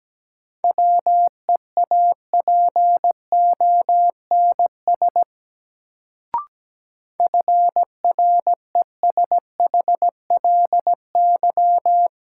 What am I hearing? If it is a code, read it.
Morse code, heard as a keyed tone, about 17 wpm, 700 Hz. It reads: WEAPONS FRESHLY
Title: Sets of 5 Words Encore - New Word Drill - 17wpm